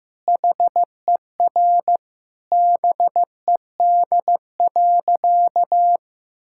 Morse code, heard as a keyed tone, about 15 wpm, 700 Hz. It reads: HER BED.
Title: Sentences from Top 1000 Words - Review Part 6 - 15wpm